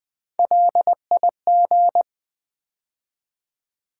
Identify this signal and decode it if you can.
Morse code, heard as a keyed tone, about 20 wpm, 700 Hz. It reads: LIG
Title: Build A Word - 3 Letters At A Time - 8-Letter Words - 20wpm